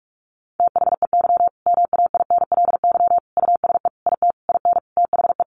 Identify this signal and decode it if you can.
Morse code, heard as a keyed tone, about 44 wpm, 700 Hz. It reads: THEY MAINLY USE IT IN THE
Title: Sentences from Top 1400 Words 45wpm